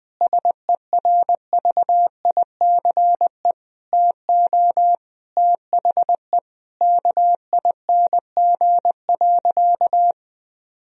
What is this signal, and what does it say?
Morse code, heard as a keyed tone, about 20 wpm, 700 Hz. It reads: SERVICE TO THE KING.
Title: Sentences from Top 1000 Words - Review Part 4 - 20wpm